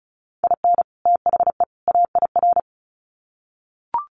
Morse code, heard as a keyed tone, about 34 wpm, 700 Hz. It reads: IN THE AIR
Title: Sets of 3 Words 35wpm